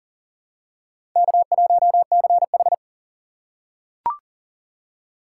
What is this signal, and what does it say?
Morse code, heard as a keyed tone, about 40 wpm, 700 Hz. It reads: K1CH
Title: USA Call Signs - 4 Characters - 4x Character Spacing - 40wpm